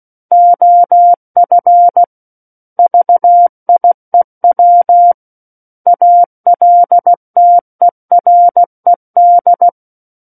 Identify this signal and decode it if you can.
Morse code, heard as a keyed tone, about 16 wpm, 700 Hz. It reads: OF VIEW ALTERED
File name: Book One / 07-Chapter 7 - 16 WPM